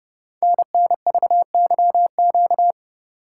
Morse code, heard as a keyed tone, about 30 wpm, 700 Hz. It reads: NNVYQ
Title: Code Groups - 5 Letters - 30wpm